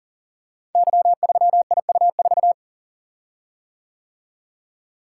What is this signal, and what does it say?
Morse code, heard as a keyed tone, about 42 wpm, 700 Hz. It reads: Y3IU4
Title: Code Groups - 5 Letter:Numbers - 40wpm